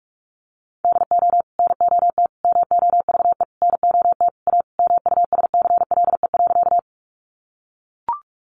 Morse code, heard as a keyed tone, about 44 wpm, 700 Hz. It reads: DO NOT MOVE NOT A MUSCLE.
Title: Sentences from Top 1300 Words 45wpm